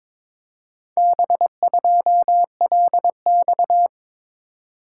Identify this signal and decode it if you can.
Morse code, heard as a keyed tone, about 22 wpm, 700 Hz. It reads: B2LX
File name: practice / 22wpm / Four Letter-Number Combinations - 22wpm